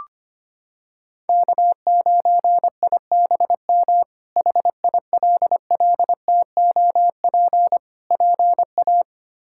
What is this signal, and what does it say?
Morse code, heard as a keyed tone, about 25 wpm, 700 Hz. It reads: K9IBM HILLTOP PA
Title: K1USN SST QSO Practice - 25wpm